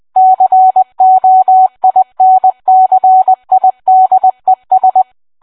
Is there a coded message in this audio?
Yes: Morse code from 0.2 to 5.0 s, heard as a keyed tone, about 20 wpm, 750 Hz. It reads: COINCIDES